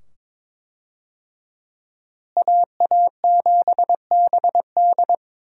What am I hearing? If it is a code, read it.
Morse code, heard as a keyed tone, about 22 wpm, 700 Hz. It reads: AA7BD
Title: USA Call Signs - 5 Characters - Repeated 2x - 22wpm